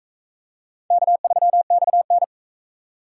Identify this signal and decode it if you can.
Morse code, heard as a keyed tone, about 42 wpm, 700 Hz. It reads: K3XN